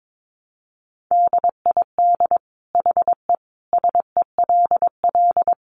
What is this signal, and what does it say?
Morse code, heard as a keyed tone, about 22 wpm, 700 Hz. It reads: DID HE SELL
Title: Sentences from Top 1300 Words 22wpm